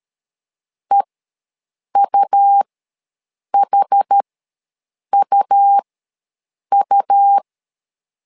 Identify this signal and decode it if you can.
Morse code, heard as a keyed tone, about 13 wpm, spaced as for 7 wpm, 800 Hz. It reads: EUHUU
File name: LEZIONI / 2 UHDN / 2 UH ws 12wpm 1730